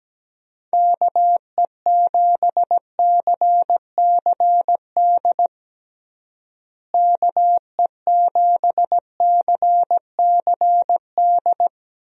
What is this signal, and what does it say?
Morse code, heard as a keyed tone, about 17 wpm, 700 Hz. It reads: KE7CCD KE7CCD
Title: USA Call Signs - 6 Characters - Repeated 2x - 17wpm